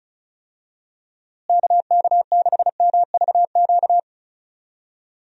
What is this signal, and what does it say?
Morse code, heard as a keyed tone, about 36 wpm, 700 Hz. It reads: KK6MVQ
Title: USA Call Signs - 6 Characters - Repeated 3x - 35wpm